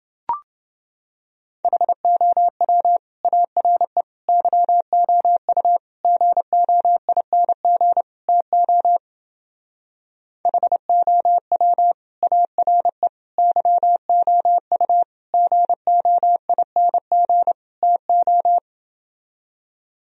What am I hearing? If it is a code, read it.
Morse code, heard as a keyed tone, about 30 wpm, 700 Hz. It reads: HOW ARE YOU GOING TO HOW ARE YOU GOING TO